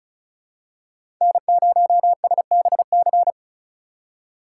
Morse code, heard as a keyed tone, about 36 wpm, 700 Hz. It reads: N0SBC
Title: USA Call Signs - 5 Characters - 4x Character Spacing - 35wpm